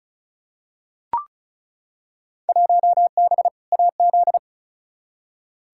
Morse code, heard as a keyed tone, about 36 wpm, 700 Hz. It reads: 1B AZ
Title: Field Day Exchanges - 35wpm